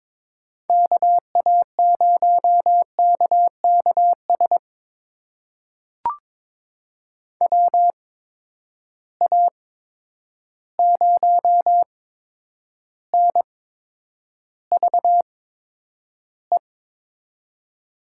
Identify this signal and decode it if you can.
Morse code, heard as a keyed tone, about 21 wpm, spaced as for 9 wpm, 700 Hz. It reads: KA0KKS WA0NVE
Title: USA Call Signs - 6 Characters - 8x Character Spacing - 22wpm